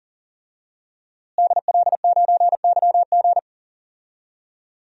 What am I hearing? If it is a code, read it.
Morse code, heard as a keyed tone, about 40 wpm, 700 Hz. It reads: DL9YG
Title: DXCC-100 Call Signs - 5 Characters - Repeated 2x - 40wpm